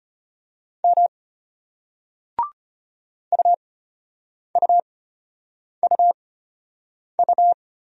Morse code, heard as a keyed tone, about 33 wpm, spaced as for 8 wpm, 700 Hz. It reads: M UUUU